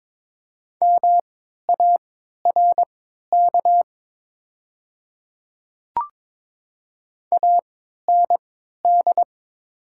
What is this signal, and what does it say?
Morse code, heard as a keyed tone, about 22 wpm, spaced as for 13 wpm, 700 Hz. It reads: MARK AND
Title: 3x Character Spacing - Top 500 Words - 22wpm